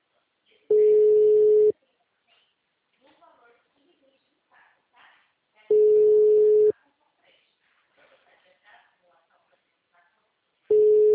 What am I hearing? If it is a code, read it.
Ringback tone (Europe, 425 Hz), with speech.